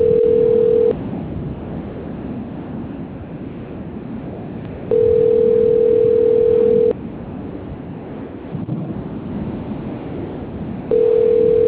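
Ringback tone (North America), 440 + 480 Hz, with noise.
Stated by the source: Outgoing telephone call from the Limeira IV line.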